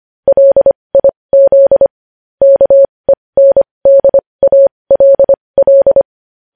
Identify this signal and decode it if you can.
Morse code, heard as a keyed tone, about 25 wpm, spaced as for 20 wpm, 550 Hz. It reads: LIZ KENDALL
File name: podcast / News headlines-1x-25x20